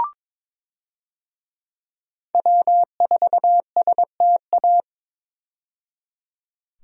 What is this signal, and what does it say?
Morse code, heard as a keyed tone, about 22 wpm, 700 Hz. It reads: W4STA
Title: USA Call Signs - 5 Characters - 22wpm